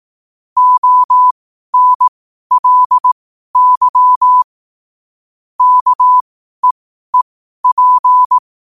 Morse code, heard as a keyed tone, about 18 wpm, spaced as for 12 wpm, 1000 Hz. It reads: ONLY KEEP
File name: qotdfiles / 12wpm